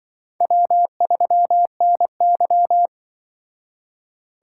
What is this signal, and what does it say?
Morse code, heard as a keyed tone, about 23 wpm, 700 Hz. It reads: W3NY
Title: Speed Racing - USA Call Signs 20wpm